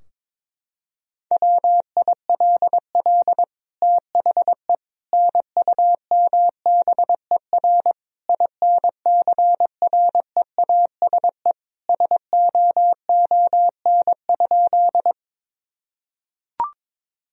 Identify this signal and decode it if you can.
Morse code, heard as a keyed tone, about 21 wpm, 700 Hz. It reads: WILL THE NUMBER INCREASE SOON?
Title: Sentences from Top 1000 Words - Review Part 4 - 22wpm